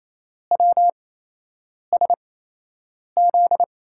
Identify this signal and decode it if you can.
Morse code, heard as a keyed tone, about 28 wpm, spaced as for 8 wpm, 700 Hz. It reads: WSZ
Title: USA Call Signs - 8x Character Spacing - 28wpm